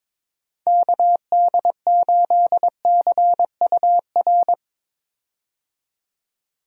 Morse code, heard as a keyed tone, about 22 wpm, 700 Hz. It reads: KD8CUR
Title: USA Call Signs - 6 Characters - 22wpm